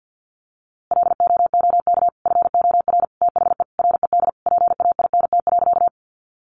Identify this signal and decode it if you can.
Morse code, heard as a keyed tone, about 50 wpm, 700 Hz. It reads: LOOK FOR THE RED PAINT.
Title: A Sound Of Thunder 50wpm